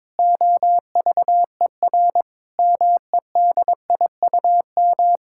Morse code, heard as a keyed tone, about 22 wpm, 700 Hz. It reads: OVER MEDIUM
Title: Sentences from Top 1000 Words - Review Part 6 - 22wpm